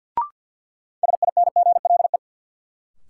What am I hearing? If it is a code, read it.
Morse code, heard as a keyed tone, about 50 wpm, 700 Hz. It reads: SINGLE
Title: Sentences from Top 700 Words - New Word Drill - 50wpm